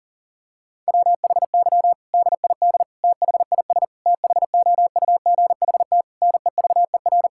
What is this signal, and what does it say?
Morse code, heard as a keyed tone, about 40 wpm, 700 Hz. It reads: WHY DID THIS THOUGHT NEVER
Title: Sentences from Top 1000 Words - Review Part 6 - 40wpm